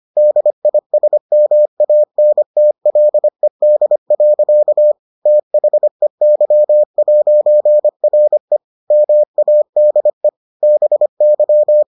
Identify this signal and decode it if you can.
Morse code, heard as a keyed tone, about 25 wpm, 600 Hz. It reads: DISMANTLED. THEY'RE MADE BY